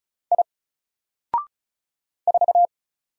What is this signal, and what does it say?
Morse code, heard as a keyed tone, about 36 wpm, 700 Hz. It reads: I 4
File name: Single-Letter-Number-35wpm